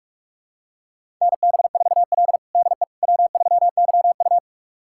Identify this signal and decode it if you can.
Morse code, heard as a keyed tone, about 46 wpm, 700 Hz. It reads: NB4L DE W3YU
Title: USA Callsign DE Callsign - 4 Characters - 45wpm